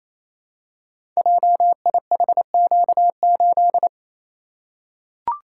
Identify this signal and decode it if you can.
Morse code, heard as a keyed tone, about 28 wpm, 700 Hz. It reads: JIHQ8
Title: Code Groups - 5 Letter:Numbers - 28wpm